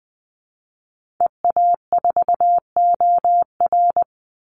Morse code, heard as a keyed tone, about 20 wpm, 700 Hz. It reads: EA4OR
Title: Call Signs Repeated 2x - 20wpm